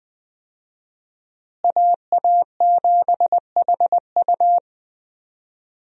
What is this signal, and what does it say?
Morse code, heard as a keyed tone, about 21 wpm, 700 Hz. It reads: AA7HU
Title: USA Call Signs - 5 Characters - Repeated 2x - 20wpm